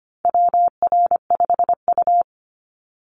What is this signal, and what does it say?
Morse code, heard as a keyed tone, about 25 wpm, 700 Hz. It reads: WR5U
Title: Call Signs Repeated 2x - 25wpm